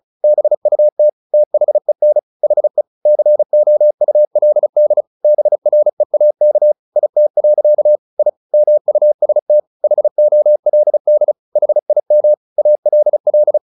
Morse code, heard as a keyed tone, about 36 wpm, 600 Hz. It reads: BUT THEN HE COULD BREAK IT. I MUST HOLD HIM ALL
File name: the old man and the sea / the old man and the sea 1 35wpm